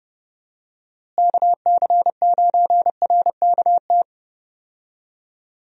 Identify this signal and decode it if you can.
Morse code, heard as a keyed tone, about 30 wpm, 700 Hz. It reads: KC9RKT
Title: USA Call Signs - 6 Characters - Repeated 2x - 30wpm